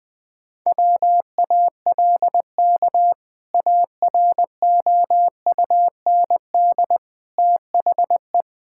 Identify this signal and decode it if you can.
Morse code, heard as a keyed tone, about 20 wpm, 700 Hz. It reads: WALK AROUND THE
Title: Sentences with 2-Word Phrases - Spoken Beginning - 20wpm